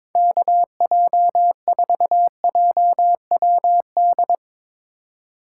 Morse code, heard as a keyed tone, about 24 wpm, 700 Hz. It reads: KJ4JWD
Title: Speed Racing - USA 2x3 Call Signs 22wpm